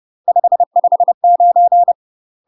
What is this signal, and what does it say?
Morse code, heard as a keyed tone, about 30 wpm, 700 Hz. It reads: 559
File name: UK MASTER QSOs TRAINING / 30WPM / UK MASTER QSOs 0040